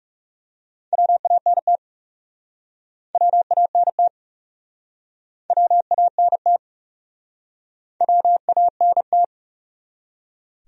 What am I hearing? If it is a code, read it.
Morse code, heard as a keyed tone, about 40 wpm, 700 Hz. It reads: WANT WANT WANT WANT